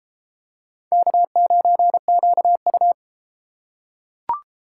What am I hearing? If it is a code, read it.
Morse code, heard as a keyed tone, about 32 wpm, 700 Hz. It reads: K9QU